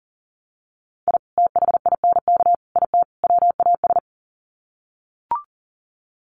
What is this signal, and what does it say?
Morse code, heard as a keyed tone, about 39 wpm, 700 Hz. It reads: I THINK IT WAS
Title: Sets of 4 Words 40wpm